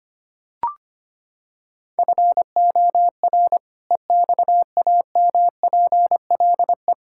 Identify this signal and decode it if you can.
Morse code, heard as a keyed tone, about 25 wpm, 700 Hz. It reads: FOR EXAMPLE
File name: practice / 25wpm / Sentences with 2-Word Phrases - 25wpm